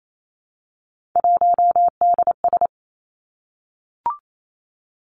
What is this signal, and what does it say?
Morse code, heard as a keyed tone, about 28 wpm, 700 Hz. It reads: 1DS
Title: Three Challenging Letter-Number Combinations 28wpm